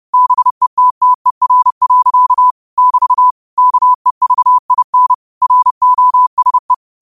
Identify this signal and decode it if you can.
Morse code, heard as a keyed tone, about 30 wpm, 1000 Hz. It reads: BETTER. <BT> KEVIN ROSE
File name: qotdfiles / 30wpm